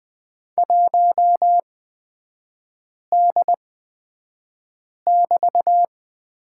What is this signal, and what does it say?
Morse code, heard as a keyed tone, about 20 wpm, spaced as for 5 wpm, 700 Hz. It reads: 1D<BT>
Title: 224 Lesson 36e - BT - Three Characters 20wpm